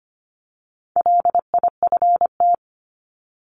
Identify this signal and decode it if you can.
Morse code, heard as a keyed tone, about 25 wpm, 700 Hz. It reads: LIFT
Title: Top 751-1000 Words 25wpm